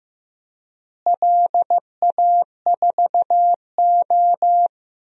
Morse code, heard as a keyed tone, about 15 wpm, 700 Hz. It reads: LA4O
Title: DXCC-100 Call Signs - 4 Characters - Repeated 3x - 15wpm